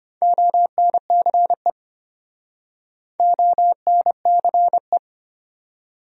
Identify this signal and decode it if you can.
Morse code, heard as a keyed tone, about 33 wpm, 700 Hz. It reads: ONCE ONCE